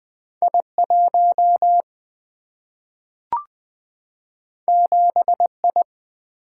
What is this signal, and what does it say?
Morse code, heard as a keyed tone, about 20 wpm, 700 Hz. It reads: I 1 7 I
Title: Single-Letter Single-Number 20wpm